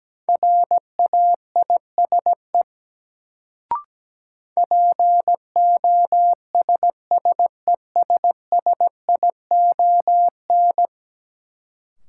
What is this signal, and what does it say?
Morse code, heard as a keyed tone, about 17 wpm, 700 Hz. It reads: RAISE POSSESSION
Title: Sentences from Top 1400 Words - New Word Drill - 17wpm